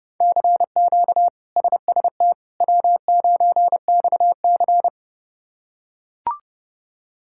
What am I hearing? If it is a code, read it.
Morse code, heard as a keyed tone, about 31 wpm, 700 Hz. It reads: CQ SST W9XC
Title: K1USN SST QSO Practice - 30wpm